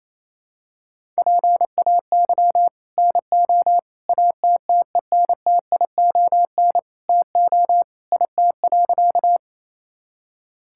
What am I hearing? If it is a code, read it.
Morse code, heard as a keyed tone, about 28 wpm, 700 Hz. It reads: PAY NO ATTENTION TO IT.
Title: Sentences from Top 1000 Words - Review Part 6 - 28wpm